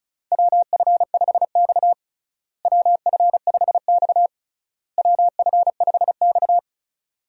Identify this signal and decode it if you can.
Morse code, heard as a keyed tone, about 36 wpm, 700 Hz. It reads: WF5X WF5X WF5X
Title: USA Call Signs - 4 Characters - Repeated 3x - 35wpm